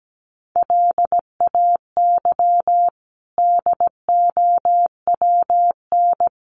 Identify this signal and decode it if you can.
Morse code, heard as a keyed tone, about 17 wpm, 700 Hz. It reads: LAY DOWN
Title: Sentences from Top 1500 Words 17wpm